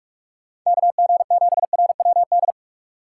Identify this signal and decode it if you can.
Morse code, heard as a keyed tone, about 43 wpm, 700 Hz. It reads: KG7RWD
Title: USA Call Signs - 4x Character Spacing - 45wpm